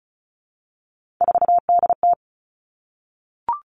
Morse code, heard as a keyed tone, about 36 wpm, 700 Hz. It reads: 4DT
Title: Three Challenging Letter-Number Combinations 35wpm